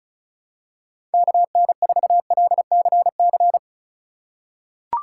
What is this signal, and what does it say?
Morse code, heard as a keyed tone, about 34 wpm, 700 Hz. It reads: KN4LCC
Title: USA Call Signs - 6 Characters - 2x Character Spacing - 35wpm